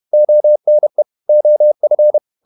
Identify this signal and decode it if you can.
Morse code, heard as a keyed tone, about 31 wpm, 600 Hz. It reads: ONE OF
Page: Never 31wpm | CW med Gnister